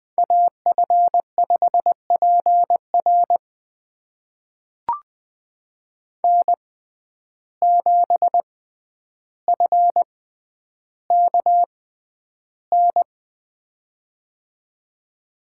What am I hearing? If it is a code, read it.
Morse code, heard as a keyed tone, about 20 wpm, spaced as for 10 wpm, 700 Hz. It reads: AF5PR N7FKN
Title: USA Call Signs - 5 Characters - 6x Character Spacing - 20wpm